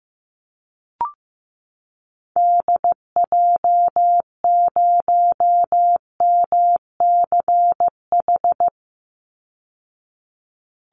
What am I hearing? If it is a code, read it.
Morse code, heard as a keyed tone, about 15 wpm, 700 Hz. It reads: DJ0MCH
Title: Call Signs Repeated 3x - 15wpm